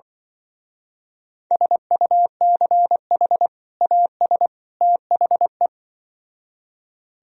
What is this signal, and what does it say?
Morse code, heard as a keyed tone, about 25 wpm, 700 Hz. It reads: SUCH AS THE
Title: Speed-Racing - Sets of 3 Words 20wpm